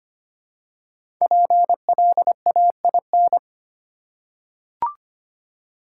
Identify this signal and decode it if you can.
Morse code, heard as a keyed tone, about 25 wpm, 700 Hz. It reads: PLAIN
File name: 5 Letter Words - First 3 Letters Given - 25wpm